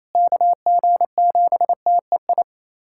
Morse code, heard as a keyed tone, about 28 wpm, 700 Hz. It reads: KG7TEI
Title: USA Callsign DE Callsign - 6 Characters - 28wpm